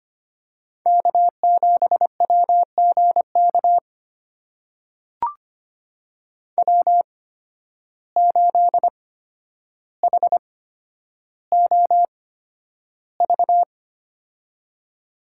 Morse code, heard as a keyed tone, about 25 wpm, spaced as for 11 wpm, 700 Hz. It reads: K7WGK W8HOV
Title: USA Call Signs - 5 Characters - 8x Character Spacing - 25wpm